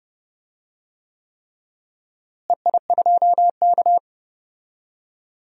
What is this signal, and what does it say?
Morse code, heard as a keyed tone, about 31 wpm, 700 Hz. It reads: EI2K
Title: DXCC-100 Call Signs - 4 Characters - Repeated 3x - 30wpm